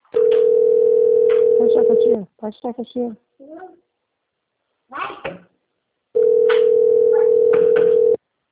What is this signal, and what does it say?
Ringback tone (North America), 440 + 480 Hz, with speech.
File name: rg-1600-unknown- - - .79719